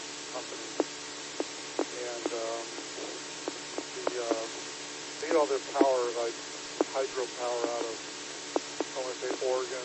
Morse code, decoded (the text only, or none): none